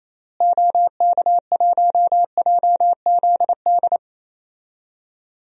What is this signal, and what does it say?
Morse code, heard as a keyed tone, about 28 wpm, 700 Hz. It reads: OK1JZD
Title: DXCC-100 Call Signs - 6 Characters - Repeated 2x - 28wpm